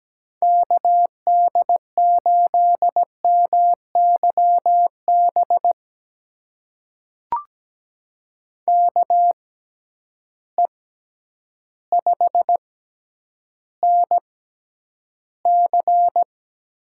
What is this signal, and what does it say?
Morse code, heard as a keyed tone, about 17 wpm, spaced as for 9 wpm, 700 Hz. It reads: KD8MYB KE5NC
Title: USA Call Signs - 6 Characters - 6x Character Spacing - 17wpm